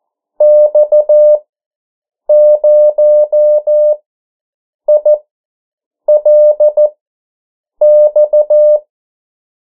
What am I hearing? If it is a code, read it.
Morse code, heard as a keyed tone, about 14 wpm, spaced as for 7 wpm, 600 Hz. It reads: X0ILX
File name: morse mp3 / 09 urok 8